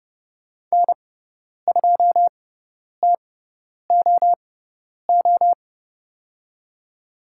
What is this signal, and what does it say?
Morse code, heard as a keyed tone, about 30 wpm, spaced as for 10 wpm, 700 Hz. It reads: N2TOO
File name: Lesson 16g - Letter c - Call Signs 30wpm